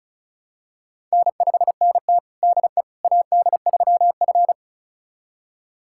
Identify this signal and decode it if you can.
Morse code, heard as a keyed tone, about 36 wpm, 700 Hz. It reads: N5NT DE AD3F
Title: USA Callsign DE Callsign - 4 Characters - 35wpm